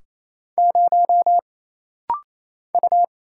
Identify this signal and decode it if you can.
Morse code, heard as a keyed tone, about 28 wpm, 700 Hz. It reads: 0 U
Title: Single Letter-Number - Rapid-Fire 28wpm